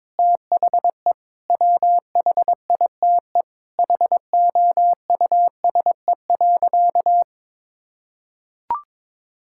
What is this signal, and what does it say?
Morse code, heard as a keyed tone, about 22 wpm, 700 Hz. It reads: THE WHITE HOUSE.
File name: practice / 22wpm / Sentences with 8-Word Phrases - 22wpm